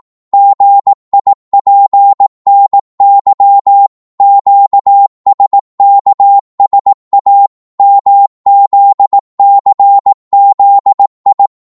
Morse code, heard as a keyed tone, about 18 wpm, 800 Hz. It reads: GIPNY QSKSA MZCZI
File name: Morse / ltr-18wpm